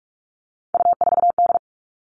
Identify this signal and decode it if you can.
Morse code, heard as a keyed tone, about 44 wpm, 700 Hz. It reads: U4D
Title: All Cut Numbers - 3-Digit 45wpm